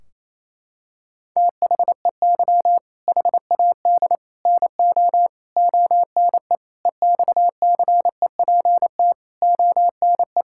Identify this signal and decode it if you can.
Morse code, heard as a keyed tone, about 28 wpm, 700 Hz. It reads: THEY HAD NO ONE EXCEPT ONE
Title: Sentences from Top 1600 Words 28wpm